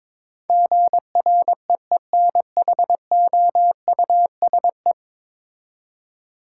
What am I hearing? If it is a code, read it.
Morse code, heard as a keyed tone, about 21 wpm, 700 Hz. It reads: GREENHOUSE
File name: practice / 22wpm / 10 Letter Words - First 3 Letters Given - 22wpm